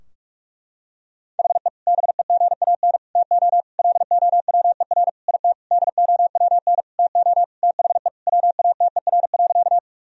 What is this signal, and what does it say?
Morse code, heard as a keyed tone, about 46 wpm, 700 Hz. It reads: HE BEGAN TO LOWER IT DOWN TO THE WATER.
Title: Sentences from Top 1000 Words - Review Part 4 - 45wpm